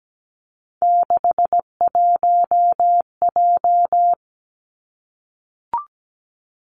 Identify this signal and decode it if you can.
Morse code, heard as a keyed tone, about 17 wpm, 700 Hz. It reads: 61J
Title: Three Challenging Letter-Number Combinations 17wpm